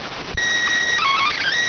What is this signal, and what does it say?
SSTV (slow-scan television), PD 290.